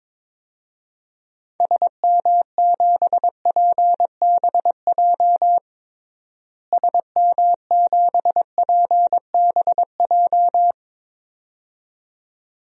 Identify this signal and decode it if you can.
Morse code, heard as a keyed tone, about 22 wpm, 700 Hz. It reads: SM7PBJ SM7PBJ
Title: DXCC-100 Call Signs - 6 Characters - Repeated 2x - 22wpm